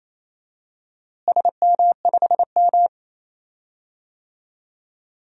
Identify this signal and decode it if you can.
Morse code, heard as a keyed tone, about 28 wpm, 700 Hz. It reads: SM5M
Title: DXCC-100 Call Signs - 4 Characters - Repeated 3x - 28wpm